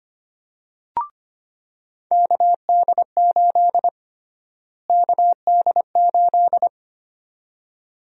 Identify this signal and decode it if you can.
Morse code, heard as a keyed tone, about 25 wpm, 700 Hz. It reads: KD8 KD8
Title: USA Call Sign Prefixes - Repeated 2x - 25wpm